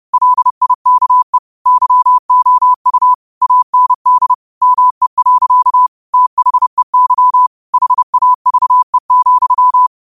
Morse code, heard as a keyed tone, about 30 wpm, 1000 Hz. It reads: LIKE YOU AND ME. THEY HAVE,
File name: qotdfiles / 30wpm